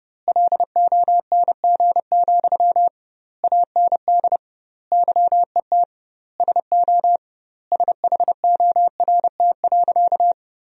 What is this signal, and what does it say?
Morse code, heard as a keyed tone, about 31 wpm, 700 Hz. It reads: LONG, AND YET SO SHORT.
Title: 2x Word-Spacing - Sentences from Top 500 Words - 30wpm